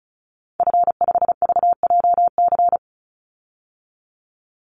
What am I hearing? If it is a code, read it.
Morse code, heard as a keyed tone, about 34 wpm, 700 Hz. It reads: F5VJC
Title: Call Signs Repeated 3x - 35wpm